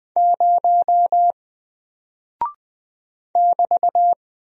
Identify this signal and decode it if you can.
Morse code, heard as a keyed tone, about 20 wpm, 700 Hz. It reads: 0 <BT>
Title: Single Character (Letter:Number:Punctuation:Prosign) Alt 20wpm